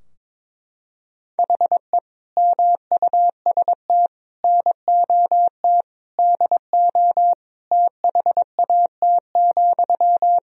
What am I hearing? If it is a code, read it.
Morse code, heard as a keyed tone, about 22 wpm, 700 Hz. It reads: HE MUST NOT DO THAT,
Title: Sentences from Top 1000 Words - Review Part 2 - 22wpm